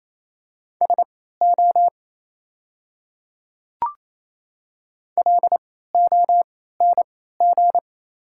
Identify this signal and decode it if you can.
Morse code, heard as a keyed tone, about 28 wpm, spaced as for 16 wpm, 700 Hz. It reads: SO LONG
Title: 3x Character Spacing - Top 100 Words - 28wpm